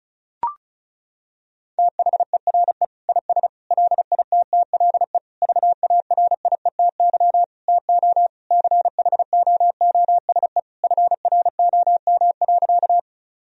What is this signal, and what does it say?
Morse code, heard as a keyed tone, about 36 wpm, 700 Hz. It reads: THERE IS LITTLE VARIETY TO CHOOSE FROM.
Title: Sentences from Top 1000 Words - Review Part 4 - 35wpm